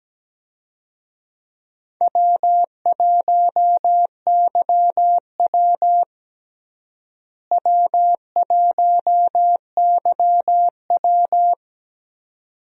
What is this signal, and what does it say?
Morse code, heard as a keyed tone, about 17 wpm, 700 Hz. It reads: W1YW W1YW
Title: USA Call Signs - 4 Characters - Repeated 2x - 17wpm